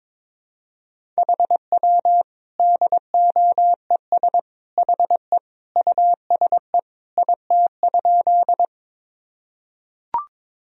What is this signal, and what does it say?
Morse code, heard as a keyed tone, about 22 wpm, 700 Hz. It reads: HW DOES HE USE IT?
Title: Sentences from Top 200 Words and Abbreviations 22wpm